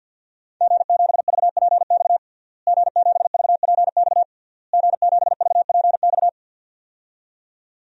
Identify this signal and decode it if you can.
Morse code, heard as a keyed tone, about 52 wpm, 700 Hz. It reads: G7VPX G7VPX G7VPX
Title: DXCC-100 Call Signs - 5 Characters - Repeated 3x - 50wpm